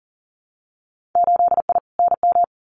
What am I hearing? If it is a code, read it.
Morse code, heard as a keyed tone, about 40 wpm, 700 Hz. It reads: 8I NM
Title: Winter Field Day Exchanges Encore 40wpm